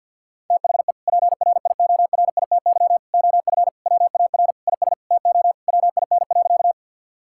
Morse code, heard as a keyed tone, about 50 wpm, 700 Hz. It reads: THE PRIORITY OF WAR IS TO WIN.
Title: Sentences from Top 1600 Words 50wpm